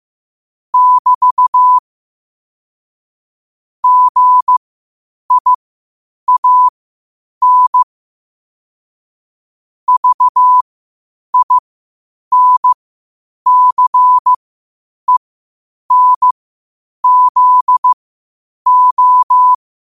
Morse code, heard as a keyed tone, about 15 wpm, spaced as for 8 wpm, 1000 Hz. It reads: <BT> GIAN VINCENZO
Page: qotd-at-8-wpm